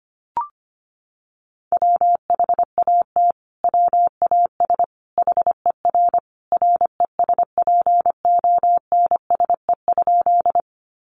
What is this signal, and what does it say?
Morse code, heard as a keyed tone, about 25 wpm, 700 Hz. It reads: WHAT WAS HER RESPONSE?
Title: Sentences from Top 1400 Words 25wpm